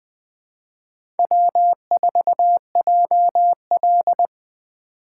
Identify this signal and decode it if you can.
Morse code, heard as a keyed tone, about 20 wpm, 700 Hz. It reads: W4JL
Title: 2023 SOTA Chaser Call Signs - 20wpm